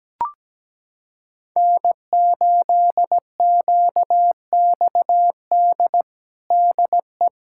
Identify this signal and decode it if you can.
Morse code, heard as a keyed tone, about 17 wpm, 700 Hz. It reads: N8QXD DE
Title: USA Callsign DE Callsign 17wpm